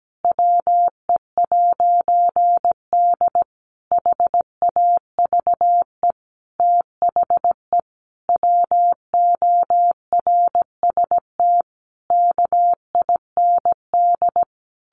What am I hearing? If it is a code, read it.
Morse code, heard as a keyed tone, about 17 wpm, 700 Hz. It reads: WE'D HAVE THE WORST KIND
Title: A Sound Of Thunder 17wpm